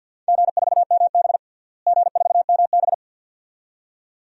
Morse code, heard as a keyed tone, about 48 wpm, 700 Hz. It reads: G4MB G4MB
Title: DXCC-100 Call Signs - 4 Characters - Repeated 2x - 50wpm